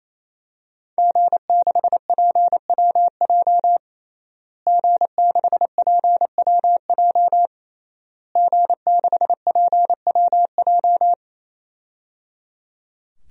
Morse code, heard as a keyed tone, about 28 wpm, 700 Hz. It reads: G6PWJ G6PWJ G6PWJ
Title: DXCC-100 Call Signs - 5 Characters - Repeated 3x - 28wpm